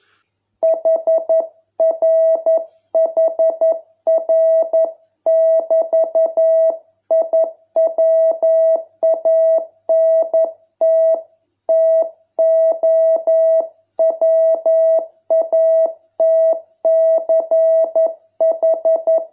Morse code, heard as a keyed tone, about 11 wpm, 660 Hz. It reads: HRHR<BT>I WANT TO WATCH